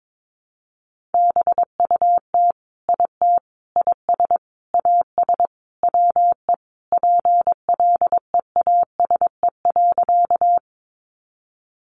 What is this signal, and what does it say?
Morse code, heard as a keyed tone, about 22 wpm, 700 Hz. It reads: BUT IT IS AS WE PLEASE.
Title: Sentences from Top 1500 Words 22wpm